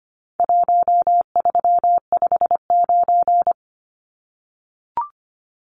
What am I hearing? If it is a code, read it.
Morse code, heard as a keyed tone, about 25 wpm, 700 Hz. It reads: 1359
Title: Numbers - Contest Style - 4-Digit 25wpm